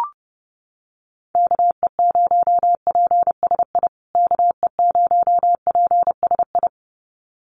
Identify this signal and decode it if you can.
Morse code, heard as a keyed tone, about 30 wpm, 700 Hz. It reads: KE0PSI KE0PSI
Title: USA Call Signs Repeated 2x 30wpm